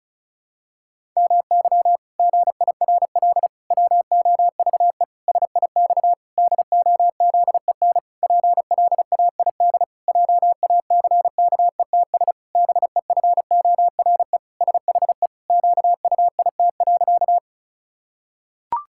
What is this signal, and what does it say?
Morse code, heard as a keyed tone, about 34 wpm, 700 Hz. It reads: MY GIRL WOVE SIX DOZEN PLAID JACKETS BEFORE SHE QUIT.
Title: Pangram Challenge v2 - 35wpm